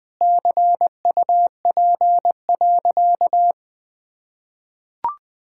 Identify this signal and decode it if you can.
Morse code, heard as a keyed tone, about 20 wpm, 700 Hz. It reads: CUP.
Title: Sentences from Top 1600 Words 20wpm